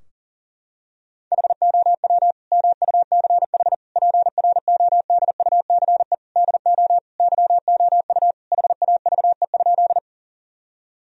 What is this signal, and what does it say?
Morse code, heard as a keyed tone, about 40 wpm, 700 Hz. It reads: HOW MUCH PRODUCE DO YOU HAVE?
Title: Sentences from Top 1000 Words - Review Part 3 - 40wpm